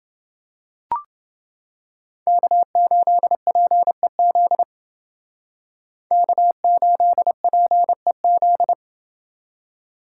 Morse code, heard as a keyed tone, about 29 wpm, 700 Hz. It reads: K8PEZ K8PEZ